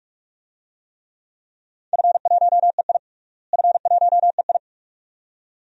Morse code, heard as a keyed tone, about 46 wpm, 700 Hz. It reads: F1EI F1EI
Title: DXCC-100 Call Signs - 4 Characters - Repeated 2x - 45wpm